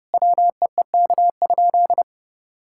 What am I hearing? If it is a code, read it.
Morse code, heard as a keyed tone, about 30 wpm, 700 Hz. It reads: WEEK?
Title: Sentences from Top 1000 Words - Review Part 5 - 30wpm